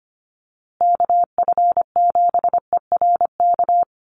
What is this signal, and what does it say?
Morse code, heard as a keyed tone, about 25 wpm, 700 Hz. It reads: KF7ERK
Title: USA Call Signs Repeated 3x 25wpm